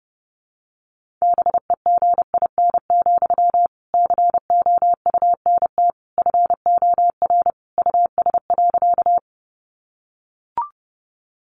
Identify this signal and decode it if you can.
Morse code, heard as a keyed tone, about 30 wpm, 700 Hz. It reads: BEGIN, COUNT FOR US.
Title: Sentences from Top 1500 Words 30wpm